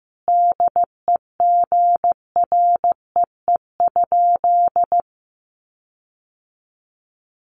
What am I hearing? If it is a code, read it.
Morse code, heard as a keyed tone, about 15 wpm, 700 Hz. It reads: DEGREE?
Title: Sentences from Top 1500 Words 15wpm